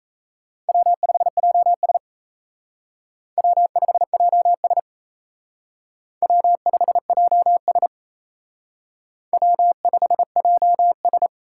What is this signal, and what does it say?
Morse code, heard as a keyed tone, about 34 wpm, 700 Hz. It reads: W5JS W5JS W5JS W5JS
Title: Speed Racing - USA 1x2 Call Signs 28wpm